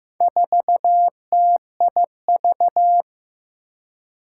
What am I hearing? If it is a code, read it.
Morse code, heard as a keyed tone, about 15 wpm, 700 Hz. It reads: 4TIV
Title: Four Challenging Letter-Number Combinations 15wpm